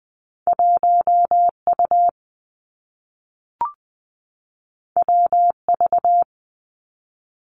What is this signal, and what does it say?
Morse code, heard as a keyed tone, about 20 wpm, 700 Hz. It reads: 1 U W V
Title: Two Challenging Letter-Number Combinations 20wpm